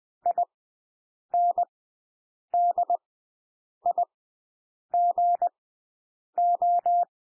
Morse code, heard as a keyed tone, about 20 wpm, spaced as for 8 wpm, 700 Hz. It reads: INDIGO
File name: cwmp3 / W1 SOUND WORDS 20 wpm